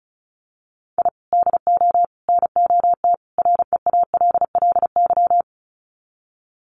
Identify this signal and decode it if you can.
Morse code, heard as a keyed tone, about 34 wpm, 700 Hz. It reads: I DO NOT REALLY
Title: Sets of 4 Words 35wpm